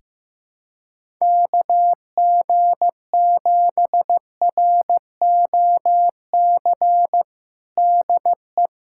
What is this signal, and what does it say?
Morse code, heard as a keyed tone, about 15 wpm, 700 Hz. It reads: KG7ROC DE
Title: USA Callsign DE Callsign - 6 Characters - 15wpm